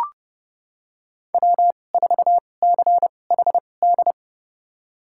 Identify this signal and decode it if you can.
Morse code, heard as a keyed tone, about 30 wpm, spaced as for 22 wpm, 700 Hz. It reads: W4CHD
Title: USA Call Signs - 5 Characters - 2x Character Spacing - 30wpm